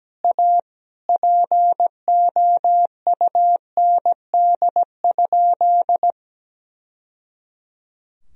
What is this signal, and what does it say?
Morse code, heard as a keyed tone, about 17 wpm, 700 Hz. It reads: A POUND?
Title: Sentences from Top 1000 Words - Review Part 2 - 17wpm